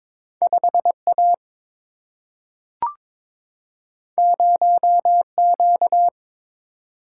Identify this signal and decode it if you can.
Morse code, heard as a keyed tone, about 21 wpm, 700 Hz. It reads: 5A 0Q
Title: Single-Letter Single-Number 22wpm